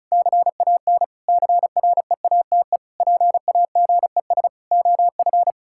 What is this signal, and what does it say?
Morse code, heard as a keyed tone, about 36 wpm, 700 Hz. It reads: CAN CREATE PAGES OF
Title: Pangram Challenge v2 - 35wpm